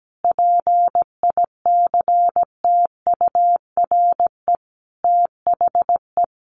Morse code, heard as a keyed tone, about 17 wpm, 700 Hz. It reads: PICTURE THE
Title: Sentences from Top 1100 Words 17wpm